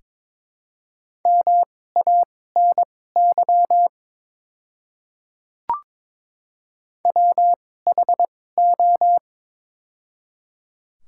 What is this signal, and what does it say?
Morse code, heard as a keyed tone, about 22 wpm, spaced as for 16 wpm, 700 Hz. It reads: MANY WHO
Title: 2x Character Spacing - Top 100 Words - 22wpm